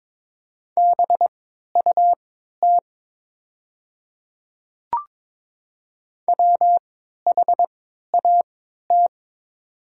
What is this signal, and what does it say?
Morse code, heard as a keyed tone, about 22 wpm, spaced as for 12 wpm, 700 Hz. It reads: BUT WHAT
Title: 3x Character Spacing - Top 100 Words - 22wpm